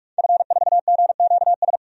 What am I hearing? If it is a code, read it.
Morse code, heard as a keyed tone, about 42 wpm, 700 Hz. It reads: F4GQS
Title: DXCC-100 Call Signs - 5 Characters - 45wpm